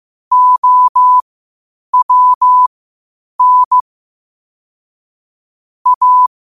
Morse code, heard as a keyed tone, about 15 wpm, spaced as for 8 wpm, 1000 Hz. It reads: OWN A
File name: qotdfiles / 8wpm